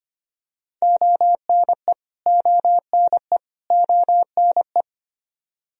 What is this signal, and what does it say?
Morse code, heard as a keyed tone, about 25 wpm, 700 Hz. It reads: ONE ONE ONE
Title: Top 100 Words Repeated 25wpm